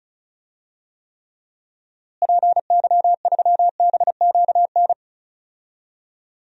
Morse code, heard as a keyed tone, about 34 wpm, 700 Hz. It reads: PY3BQN
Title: DXCC-100 Call Signs - 6 Characters - Repeated 2x - 35wpm